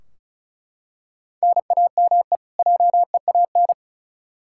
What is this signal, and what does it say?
Morse code, heard as a keyed tone, about 34 wpm, 700 Hz. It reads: NAME JEAN
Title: Practice QSO - Exchange 1 - Beginning - 35wpm